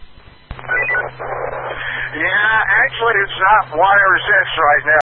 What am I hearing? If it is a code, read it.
Emergency Alert System (SAME) tones, heard as harsh buzzing bursts, with speech. It is heard from 0.6 to 4.5 s.